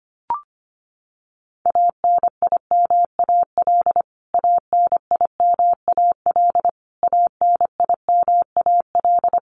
Morse code, heard as a keyed tone, about 25 wpm, 700 Hz. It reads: ANIMAL ANIMAL ANIMAL
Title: Top 500 Words Repeated 25wpm